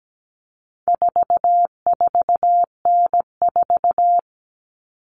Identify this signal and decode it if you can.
Morse code, heard as a keyed tone, about 17 wpm, 700 Hz. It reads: 44N4
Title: All Cut Numbers - 4-Digit 17wpm